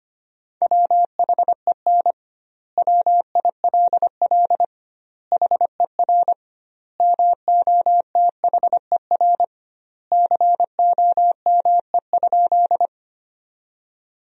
Morse code, heard as a keyed tone, about 25 wpm, 700 Hz. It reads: WHEN WILL HER MOTHER COME?
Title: 2x Word-Spacing - Sentences from Top 200 Words - 25wpm